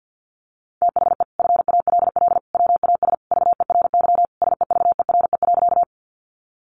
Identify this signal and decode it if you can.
Morse code, heard as a keyed tone, about 50 wpm, 700 Hz. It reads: THE FALL WAS VERY SEVERE.
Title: Sentences from Top 1400 Words 50wpm